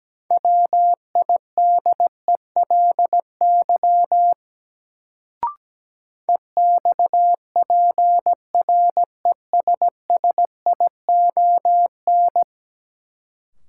Morse code, heard as a keyed tone, about 17 wpm, 700 Hz. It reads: WIDELY EXPRESSION
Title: Sentences from Top 1200 Words - New Word Drill - 17wpm